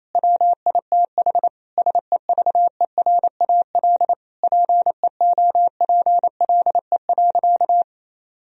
Morse code, heard as a keyed tone, about 28 wpm, 700 Hz. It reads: WITH SEVERAL PEOPLE.
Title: Sentences from Top 1000 Words - Review Part 3 - 28wpm